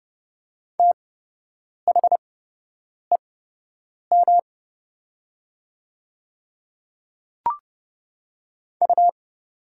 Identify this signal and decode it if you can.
Morse code, heard as a keyed tone, about 30 wpm, spaced as for 8 wpm, 700 Hz. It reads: THEM U
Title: 8x Character Spacing - Top 100 Words - 30wpm